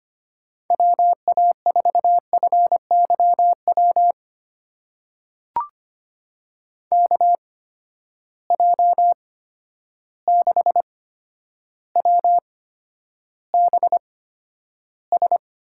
Morse code, heard as a keyed tone, about 25 wpm, spaced as for 11 wpm, 700 Hz. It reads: WA4FYW KJ6WBS
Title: USA Call Signs - 6 Characters - 8x Character Spacing - 25wpm